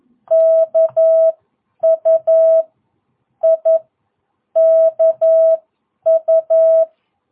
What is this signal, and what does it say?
Morse code, heard as a keyed tone, about 11 wpm, spaced as for 9 wpm, 660 Hz. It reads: KU I KU
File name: jh8cbh / 2022cw